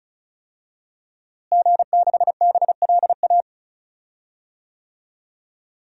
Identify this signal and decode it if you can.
Morse code, heard as a keyed tone, about 34 wpm, 700 Hz. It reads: G6BLA
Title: DXCC-100 Call Signs - 5 Characters - Repeated 2x - 35wpm